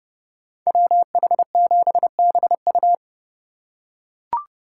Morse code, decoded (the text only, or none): WH7BU